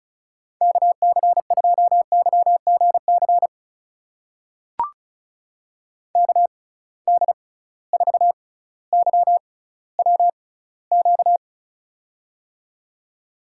Morse code, heard as a keyed tone, about 34 wpm, spaced as for 18 wpm, 700 Hz. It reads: KC2YGC KD4YWQ
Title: USA Call Signs - 6 Characters - 6x Character Spacing - 35wpm